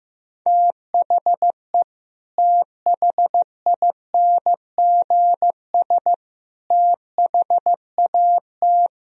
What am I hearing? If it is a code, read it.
Morse code, heard as a keyed tone, about 15 wpm, 700 Hz. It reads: THE THINGS THAT
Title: Speed-Racing - Sets of 3 Words 15wpm